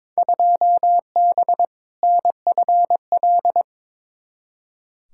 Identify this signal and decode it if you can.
Morse code, heard as a keyed tone, about 21 wpm, 700 Hz. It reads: 2B NFL
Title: Field Day Exchanges Encore - 22wpm